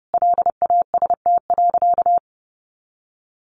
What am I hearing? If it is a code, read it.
Morse code, heard as a keyed tone, about 30 wpm, 700 Hz. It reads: LAST.
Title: Sentences from Top 300 Words 30wpm